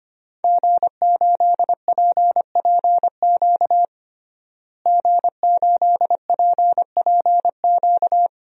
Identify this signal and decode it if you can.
Morse code, heard as a keyed tone, about 25 wpm, 700 Hz. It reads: G8PPQ G8PPQ
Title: DXCC-100 Call Signs - 5 Characters - Repeated 3x - 25wpm